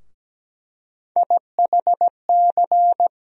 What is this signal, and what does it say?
Morse code, heard as a keyed tone, about 17 wpm, 700 Hz. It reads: IHC
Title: USA Call Sign Postfixes - 3 Letters 17wpm